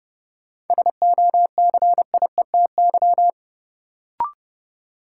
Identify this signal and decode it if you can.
Morse code, heard as a keyed tone, about 30 wpm, 700 Hz. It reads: SOCIETY
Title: Sentences from Top 700 Words - New Word Drill - 30wpm